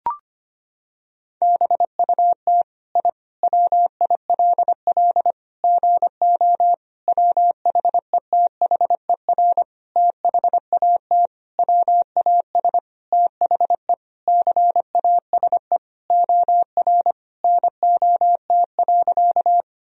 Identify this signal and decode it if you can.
Morse code, heard as a keyed tone, about 25 wpm, 700 Hz. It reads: BUT I WILL GO WHETHER THAT WAS THE CASE OR NOT.
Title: Sentences from Top 1000 Words - Review Part 3 - 25wpm